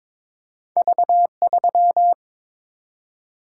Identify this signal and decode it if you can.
Morse code, heard as a keyed tone, about 22 wpm, 700 Hz. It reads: V3
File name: practice / 22wpm / Two Letter-Number Combinations - 22wpm